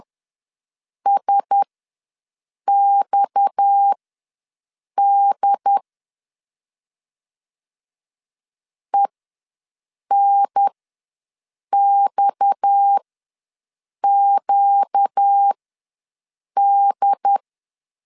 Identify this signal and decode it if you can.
Morse code, heard as a keyed tone, about 10 wpm, spaced as for 6 wpm, 800 Hz. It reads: SXD ENXQD
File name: LEZIONI / 6 PKX0 / 6 PKX ws 10wpm 3706